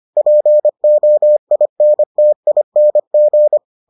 Morse code, heard as a keyed tone, about 25 wpm, 600 Hz. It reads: POINTING